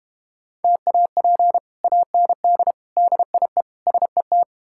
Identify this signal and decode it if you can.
Morse code, heard as a keyed tone, about 32 wpm, 700 Hz. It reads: TAP AND DIE SET